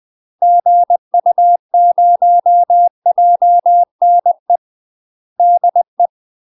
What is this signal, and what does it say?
Morse code, heard as a keyed tone, about 20 wpm, 700 Hz. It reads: GU0JNE DE
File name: UK MASTER QSOs TRAINING / 20WPM / UK MASTER QSOs 0023